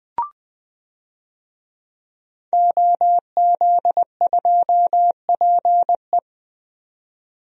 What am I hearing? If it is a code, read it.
Morse code, heard as a keyed tone, about 20 wpm, 700 Hz. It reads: OZ2PE
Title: DXCC-100 Call Signs - 5 Characters - 20wpm